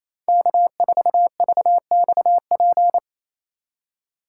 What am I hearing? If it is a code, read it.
Morse code, heard as a keyed tone, about 27 wpm, 700 Hz. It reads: K4VXP
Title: USA Call Signs - 5 Characters - 4x Character Spacing - 28wpm